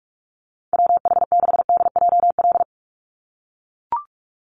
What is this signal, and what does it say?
Morse code, heard as a keyed tone, about 44 wpm, 700 Hz. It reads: WH6DJL
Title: USA Call Signs 45wpm